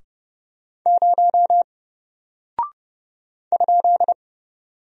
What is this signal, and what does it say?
Morse code, heard as a keyed tone, about 31 wpm, 700 Hz. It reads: 0 ?
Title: Single Character (Letter:Number:Punctuation:Prosign) 30wpm